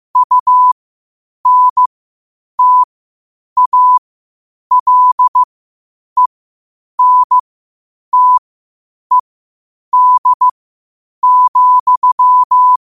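Morse code, heard as a keyed tone, about 15 wpm, spaced as for 8 wpm, 1000 Hz. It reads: UNTALENTED,
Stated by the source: Morse code quotes at 8 WPM.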